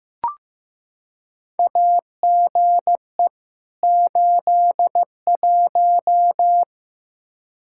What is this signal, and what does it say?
Morse code, heard as a keyed tone, about 15 wpm, 700 Hz. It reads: AGE 81
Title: Practice QSO - Exchange 3 - More Info - 15wpm